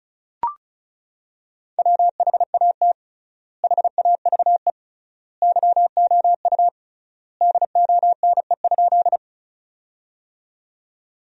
Morse code, heard as a keyed tone, about 36 wpm, 700 Hz. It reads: WHAT HAVE YOU DONE?
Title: 3x Word-Spacing - Sentences from Top 500 Words - 35wpm